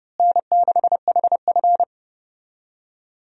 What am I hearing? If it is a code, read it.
Morse code, heard as a keyed tone, about 31 wpm, 700 Hz. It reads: N6HF
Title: USA Call Signs - 4 Characters - Repeated 2x - 30wpm